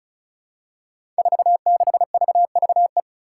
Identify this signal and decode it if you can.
Morse code, heard as a keyed tone, about 36 wpm, 700 Hz. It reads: 46VVE
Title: Five Challenging Letter-Number Combinations 35wpm